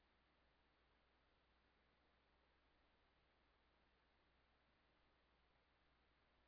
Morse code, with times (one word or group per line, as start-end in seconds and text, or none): none